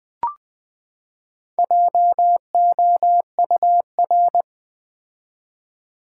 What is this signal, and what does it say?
Morse code, heard as a keyed tone, about 20 wpm, 700 Hz. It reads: JOUR
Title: Build A Word - 4 Letters At A Time - 10-Letter Words - 20wpm